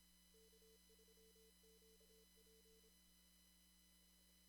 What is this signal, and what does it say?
Morse code, heard as a keyed tone, about 26 wpm, 440 Hz. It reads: K3QP
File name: CW / kt5j wpx cw-op-k5pi